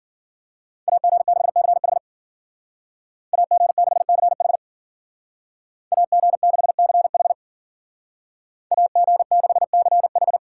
Morse code, heard as a keyed tone, about 49 wpm, 700 Hz. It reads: AG6CH AG6CH AG6CH AG6CH
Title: Speed Racing - USA Call Signs 40wpm